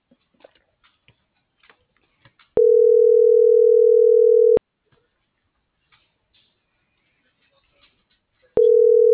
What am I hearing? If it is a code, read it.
Ringback tone (North America), 440 + 480 Hz.
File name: CENTRAL MULTIMODAL / out viridiana.juarez - .83274